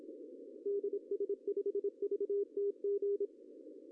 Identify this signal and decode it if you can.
Morse code, heard as a keyed tone, about 27 wpm, 410 Hz. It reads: DS5VTG
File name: CW / cqww-cw-k5tr